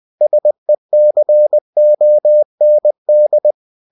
Morse code, heard as a keyed tone, about 20 wpm, 600 Hz. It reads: SECOND
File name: cw-books / winnie-the-pooh / winnie-20wpm-page-0119